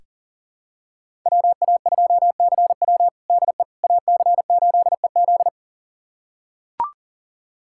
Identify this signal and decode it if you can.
Morse code, heard as a keyed tone, about 40 wpm, 700 Hz. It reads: WA2CW DE AC8EZ
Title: USA Callsign DE Callsign - 5 Characters - 40wpm